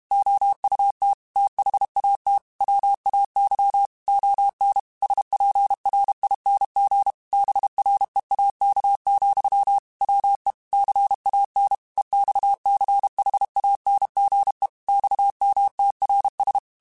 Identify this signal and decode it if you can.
Morse code, heard as a keyed tone, about 32 wpm, 800 Hz. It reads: OUT THAT WAY ON SPRING BREAK, WE CAN EXCHANGE XMTRS
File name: QSOs / QSOs-mp3s / WA7QOX 1@30